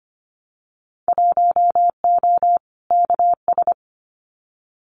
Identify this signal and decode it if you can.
Morse code, heard as a keyed tone, about 25 wpm, 700 Hz. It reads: 1O KS
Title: Winter Field Day Exchanges 25wpm